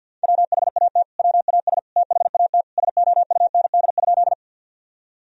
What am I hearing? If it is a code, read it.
Morse code, heard as a keyed tone, about 52 wpm, 700 Hz. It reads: WHAT WAS THAT SOUND?
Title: Sentences from Top 1000 Words - Review Part 1 - 50wpm